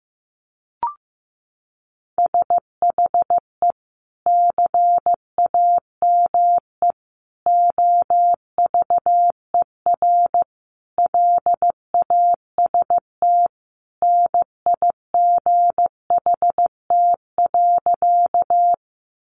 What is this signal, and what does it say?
Morse code, heard as a keyed tone, about 15 wpm, 700 Hz. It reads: SHE CAME OVER LAST NIGHT.
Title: Sentences from Top 300 Words 15wpm